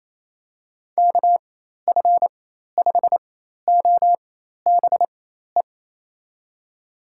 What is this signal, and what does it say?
Morse code, heard as a keyed tone, about 28 wpm, spaced as for 13 wpm, 700 Hz. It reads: KF5OBE
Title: USA Call Signs - 6 Characters - 4x Character Spacing - 28wpm